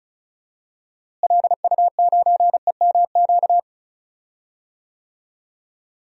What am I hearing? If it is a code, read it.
Morse code, heard as a keyed tone, about 34 wpm, 700 Hz. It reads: LU9EMQ
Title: DXCC-100 Call Signs - 6 Characters - Repeated 2x - 35wpm